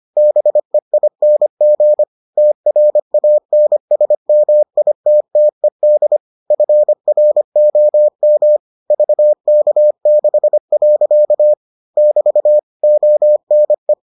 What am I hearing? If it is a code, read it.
Morse code, heard as a keyed tone, about 25 wpm, 600 Hz. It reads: BEING TRANSMITTED FROM VK6. <BT> ONE